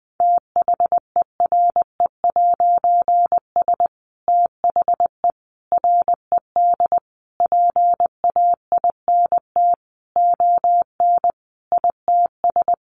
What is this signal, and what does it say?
Morse code, heard as a keyed tone, about 20 wpm, 700 Hz. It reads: THERE'S THE RED PAINT ON ITS
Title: A Sound Of Thunder 20wpm